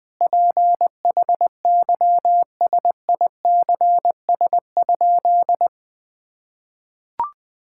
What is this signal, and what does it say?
Morse code, heard as a keyed tone, about 20 wpm, 700 Hz. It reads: PHYSICS?
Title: Sentences from Top 1000 Words - Review Part 5 - 20wpm